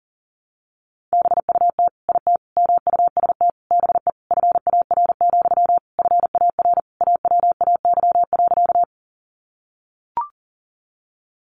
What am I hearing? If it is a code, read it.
Morse code, heard as a keyed tone, about 40 wpm, 700 Hz. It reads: BUT IT MUST BE FAR, FAR AWAY.
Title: Sentences from Top 600 Words 40wpm